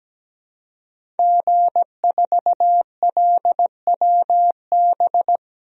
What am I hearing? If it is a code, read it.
Morse code, heard as a keyed tone, about 17 wpm, 700 Hz. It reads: G4LWB
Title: DXCC-100 Call Signs - 5 Characters - 17wpm